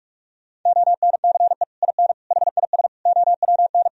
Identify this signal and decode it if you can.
Morse code, heard as a keyed tone, about 47 wpm, 700 Hz. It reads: ONCE IN HIS OWN
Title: Sentences from Top 1000 Words - Review Part 2 - 45wpm